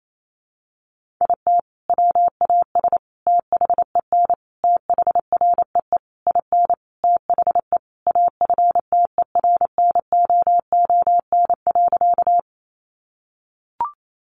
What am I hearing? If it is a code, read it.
Morse code, heard as a keyed tone, about 28 wpm, 700 Hz. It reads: IT WAS THEN THREE IN THE AFTERNOON.
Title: Sentences from Top 1500 Words 28wpm